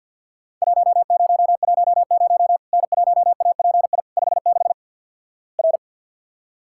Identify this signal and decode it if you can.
Morse code, heard as a keyed tone, about 48 wpm, 700 Hz. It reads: 1010 N1API 56 R
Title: Worked All Europe DX Contest - QTC Exchange with Repeats - 50wpm